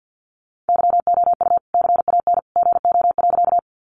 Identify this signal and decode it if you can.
Morse code, heard as a keyed tone, about 50 wpm, 700 Hz. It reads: YOU CAN GO.
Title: Sentences from Top 100 Words 50wpm